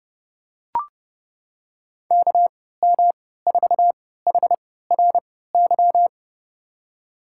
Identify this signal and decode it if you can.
Morse code, heard as a keyed tone, about 30 wpm, spaced as for 17 wpm, 700 Hz. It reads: KM4HRY
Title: USA Call Signs - 6 Characters - 3x Character Spacing - 30wpm